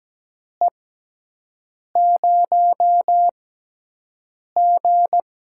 Morse code, heard as a keyed tone, about 17 wpm, spaced as for 6 wpm, 700 Hz. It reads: E0G
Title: USA Call Signs - 4 Characters - 6x Character Spacing - 17wpm